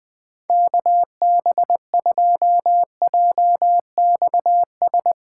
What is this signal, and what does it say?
Morse code, heard as a keyed tone, about 20 wpm, 700 Hz. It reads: KB2JXS
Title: USA Callsign DE Callsign 20wpm